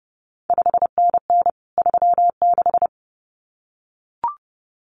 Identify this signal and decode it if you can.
Morse code, heard as a keyed tone, about 30 wpm, 700 Hz. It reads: 5NN 36
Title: CQ WW DX Contest Exchange Encore - 30wpm